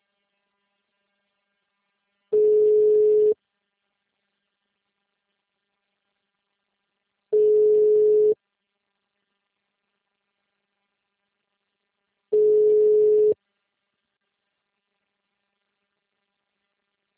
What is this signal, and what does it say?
Ringback tone (Europe, 425 Hz).